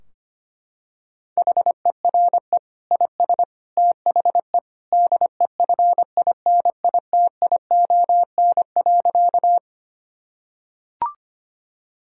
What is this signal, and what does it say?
Morse code, heard as a keyed tone, about 25 wpm, 700 Hz. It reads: HERE IS THE DEFINITION.
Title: Sentences from Top 1000 Words - Review Part 4 - 25wpm